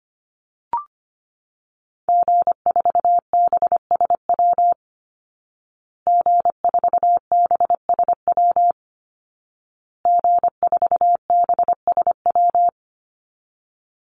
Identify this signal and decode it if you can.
Morse code, heard as a keyed tone, about 25 wpm, 700 Hz. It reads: G4BSW G4BSW G4BSW
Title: Call Signs Repeated 2x - 25wpm